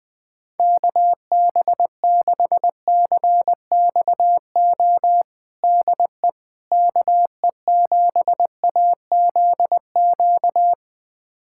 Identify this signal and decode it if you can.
Morse code, heard as a keyed tone, about 20 wpm, 700 Hz. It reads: KB6CXO DE KE7AZQ
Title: USA Callsign DE Callsign - 6 Characters - 20wpm